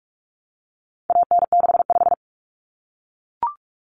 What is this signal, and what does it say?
Morse code, heard as a keyed tone, about 44 wpm, 700 Hz. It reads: AN65
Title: Common Cut Numbers - 4-Digit 45wpm